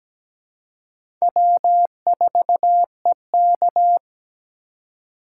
Morse code, heard as a keyed tone, about 20 wpm, 700 Hz. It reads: W4EK
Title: Speed Racing - USA 1x2 Call Signs 17wpm